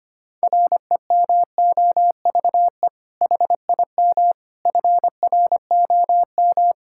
Morse code, heard as a keyed tone, about 25 wpm, 700 Hz. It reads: REMOVE HIM FROM
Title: Sentences from Top 1000 Words - Review Part 5 - 25wpm